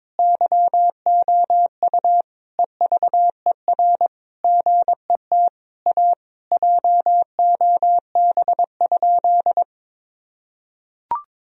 Morse code, heard as a keyed tone, about 21 wpm, 700 Hz. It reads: YOU EVER GET A JOB?
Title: Sentences from Top 1000 Words - Review Part 3 - 22wpm